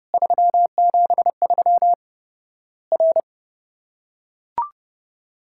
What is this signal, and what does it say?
Morse code, heard as a keyed tone, about 30 wpm, 700 Hz. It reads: 373 R
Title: Worked All Europe DX Contest - QTC Exchange - 30wpm